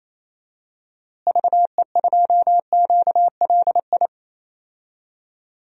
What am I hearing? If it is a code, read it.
Morse code, heard as a keyed tone, about 28 wpm, 700 Hz. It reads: VE2QLI
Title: DXCC-100 Call Signs - 6 Characters - Repeated 3x - 28wpm